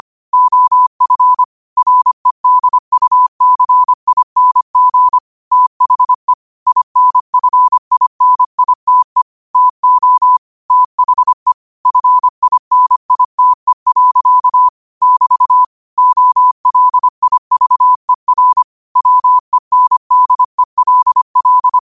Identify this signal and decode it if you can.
Morse code, heard as a keyed tone, about 25 wpm, 1000 Hz. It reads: OF REDUCING THE INFINITE TO THE FINITE. <BT> OLIVER WENDELL
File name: qotdfiles / 25wpm